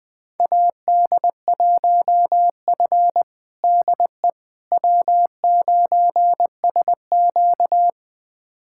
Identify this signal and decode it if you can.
Morse code, heard as a keyed tone, about 20 wpm, 700 Hz. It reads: AD1F DE W9SQ
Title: USA Callsign DE Callsign - 4 Characters - 20wpm